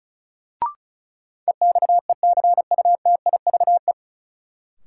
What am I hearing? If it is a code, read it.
Morse code, heard as a keyed tone, about 36 wpm, 700 Hz. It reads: EXECUTIVE
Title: Sets of 5 Words Encore - New Word Drill - 35wpm